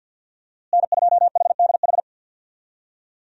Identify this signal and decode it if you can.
Morse code, heard as a keyed tone, about 49 wpm, 700 Hz. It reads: N2HDH
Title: USA Call Signs - 5 Characters - Repeated 2x - 50wpm